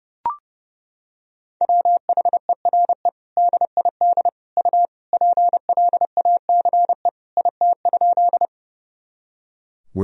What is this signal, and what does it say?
Morse code, heard as a keyed tone, about 30 wpm, 700 Hz. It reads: WHERE DID U PLACE IT?
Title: Sentences from Top 200 Words and Abbreviations 30wpm